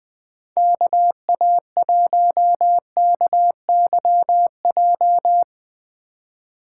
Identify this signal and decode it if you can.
Morse code, heard as a keyed tone, about 21 wpm, 700 Hz. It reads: KA1KYJ
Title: Speed Racing - USA Call Signs 20wpm